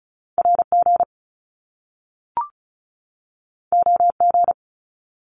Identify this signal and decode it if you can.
Morse code, heard as a keyed tone, about 36 wpm, 700 Hz. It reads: RG OG
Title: Top Two Letter Combinations in Words 35wpm